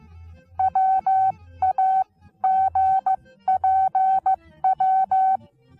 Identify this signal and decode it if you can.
Morse code, heard as a keyed tone, about 15 wpm, spaced as for 13 wpm, 760 Hz. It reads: WAGPW
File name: Onlinelehrgang / morsen / DJ4UF Morsen19